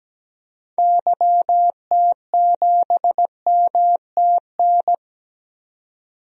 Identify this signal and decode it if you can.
Morse code, heard as a keyed tone, about 17 wpm, 700 Hz. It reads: YT7MTN
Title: DXCC-100 Call Signs - 6 Characters - 17wpm